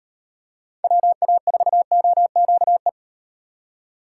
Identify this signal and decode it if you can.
Morse code, heard as a keyed tone, about 35 wpm, 700 Hz. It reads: WA4OQE